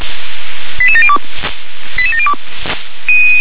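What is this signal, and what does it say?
Five-tone selective call (ZVEI), heard as quick runs of five tones, tones of about 70 ms. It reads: 88702 88702
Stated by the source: ZVEI five-tone selective call received on an SDR.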